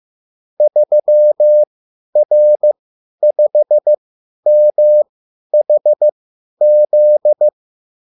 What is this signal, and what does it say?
Morse code, heard as a keyed tone, about 15 wpm, spaced as for 10 wpm, 600 Hz. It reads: 3R5MHZ